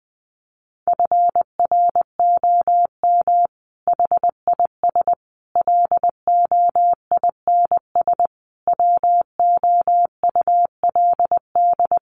Morse code, heard as a keyed tone, about 20 wpm, 700 Hz. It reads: FROM HIS LOINS WOULD
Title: A Sound Of Thunder 20wpm